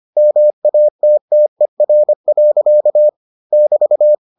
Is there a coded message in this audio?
Yes: Morse code from 0.2 to 4.1 s, heard as a keyed tone, about 25 wpm, 600 Hz. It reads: MATTER. <BT>